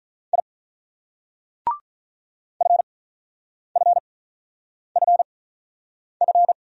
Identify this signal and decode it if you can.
Morse code, heard as a keyed tone, about 44 wpm, spaced as for 9 wpm, 700 Hz. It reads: I FFFF